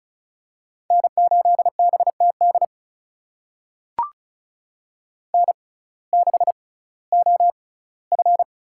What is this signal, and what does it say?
Morse code, heard as a keyed tone, about 36 wpm, spaced as for 19 wpm, 700 Hz. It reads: N8BTD N6OF
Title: USA Call Signs - 5 Characters - 6x Character Spacing - 35wpm